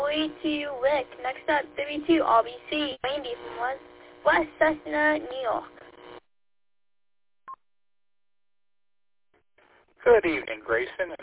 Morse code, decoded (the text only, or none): none